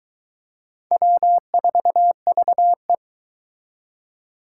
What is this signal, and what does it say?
Morse code, heard as a keyed tone, about 24 wpm, 700 Hz. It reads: W4VE